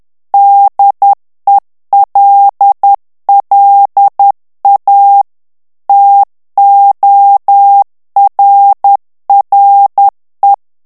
Morse code, heard as a keyed tone, about 10 wpm, 800 Hz. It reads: DELLA TORRE
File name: morse mp3 / ITALIEN texte clair 1 vitesse 10 mots minute